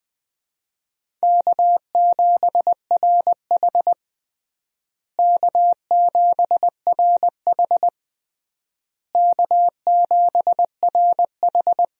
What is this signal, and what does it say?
Morse code, heard as a keyed tone, about 20 wpm, 700 Hz. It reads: K7RH K7RH K7RH
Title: USA Call Signs - 4 Characters - Repeated 3x - 20wpm